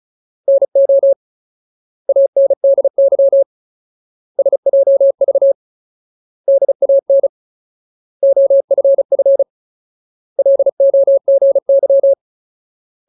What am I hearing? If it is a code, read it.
Morse code, heard as a keyed tone, about 34 wpm, 550 Hz. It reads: NO ANDY SJV DAN OFF LOGY
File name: Ultimate Word List / 35WPM Ultimate